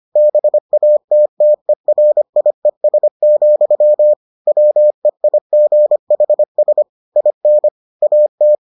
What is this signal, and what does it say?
Morse code, heard as a keyed tone, about 25 wpm, 600 Hz. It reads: BATTERIES, WEIGHS IN AT